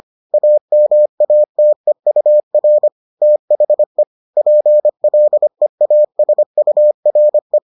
Morse code, heard as a keyed tone, about 25 wpm, 600 Hz. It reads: AMATEUR THE PLEASURE